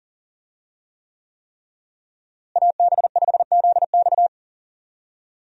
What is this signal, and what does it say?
Morse code, heard as a keyed tone, about 40 wpm, 700 Hz. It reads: AB5ZX
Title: USA Call Signs - 5 Characters - Repeated 3x - 40wpm